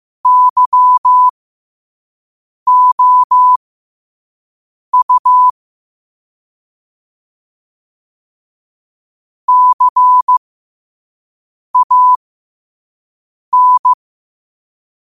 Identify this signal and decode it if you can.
Morse code, heard as a keyed tone, about 15 wpm, spaced as for 5 wpm, 1000 Hz. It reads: YOU CAN
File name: qotdfiles / 5wpm